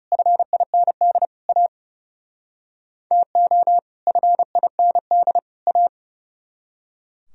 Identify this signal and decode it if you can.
Morse code, heard as a keyed tone, about 37 wpm, 700 Hz. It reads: FIND A TO FIND A
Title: Speed-Racing - Sets of 3 Words 30wpm